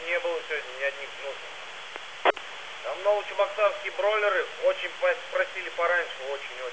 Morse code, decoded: none

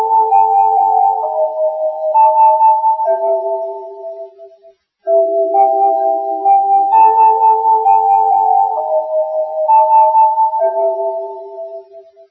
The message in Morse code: none

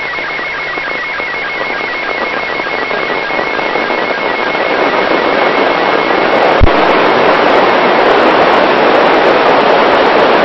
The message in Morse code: none

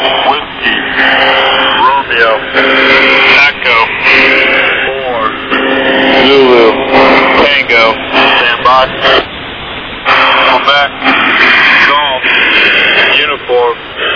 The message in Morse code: none

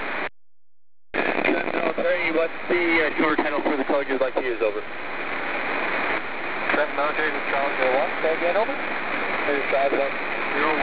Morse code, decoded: none

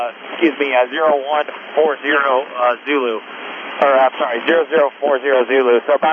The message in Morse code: none

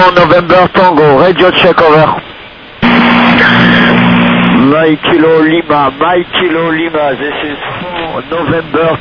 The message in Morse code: none

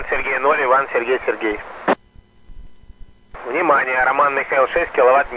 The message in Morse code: none

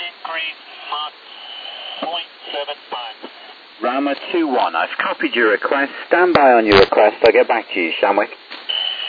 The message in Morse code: none

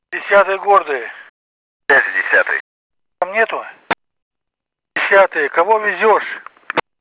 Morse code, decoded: none

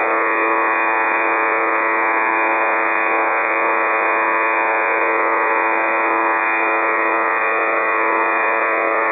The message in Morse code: none

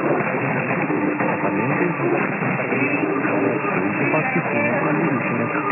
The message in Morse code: none

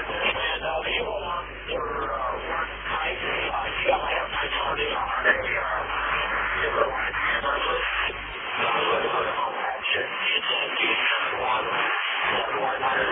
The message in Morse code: none

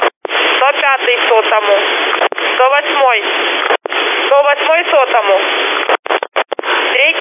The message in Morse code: none

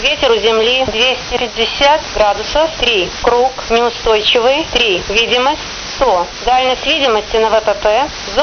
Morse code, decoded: none

